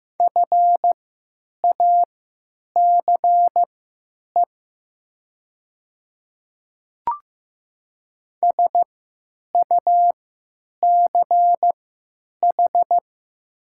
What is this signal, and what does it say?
Morse code, heard as a keyed tone, about 15 wpm, spaced as for 9 wpm, 700 Hz. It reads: FACE SUCH
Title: 3x Character Spacing - Top 300 Words - 15wpm